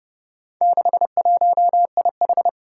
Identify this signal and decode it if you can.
Morse code, heard as a keyed tone, about 30 wpm, 700 Hz. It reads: 61IH
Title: Four Challenging Letter-Number Combinations 30wpm